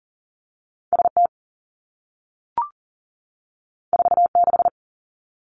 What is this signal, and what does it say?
Morse code, heard as a keyed tone, about 42 wpm, 700 Hz. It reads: ST 46
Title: Two Challenging Letter-Number Combinations 40wpm